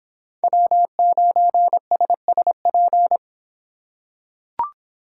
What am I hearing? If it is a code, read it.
Morse code, heard as a keyed tone, about 23 wpm, 700 Hz. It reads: W9SSP